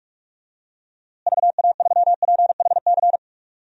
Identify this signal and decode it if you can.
Morse code, heard as a keyed tone, about 46 wpm, 700 Hz. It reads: VA3PHC
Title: DXCC-100 Call Signs - 6 Characters - Repeated 3x - 45wpm